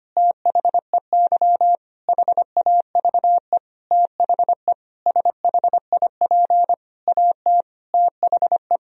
Morse code, heard as a keyed tone, about 25 wpm, 700 Hz. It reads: THEY HAVE THE SHIP AT THE
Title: Sentences from Top 1000 Words - Review Part 2 - 25wpm